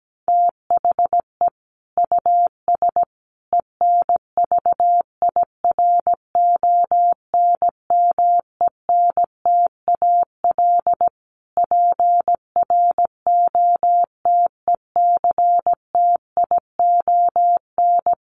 Morse code, heard as a keyed tone, about 17 wpm, 700 Hz. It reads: THE US ENVIRONMENTAL PROTECTION
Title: Sets of 5 Words Encore2 17wpm